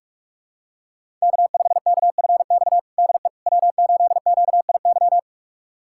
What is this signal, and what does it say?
Morse code, heard as a keyed tone, about 41 wpm, 700 Hz. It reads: K5KFX DE W8QIY
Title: USA Callsign DE Callsign - 5 Characters - 45wpm